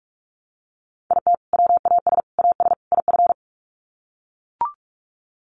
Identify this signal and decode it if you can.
Morse code, heard as a keyed tone, about 44 wpm, 700 Hz. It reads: IT WAS AS IF
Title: Sets of 4 Words 45wpm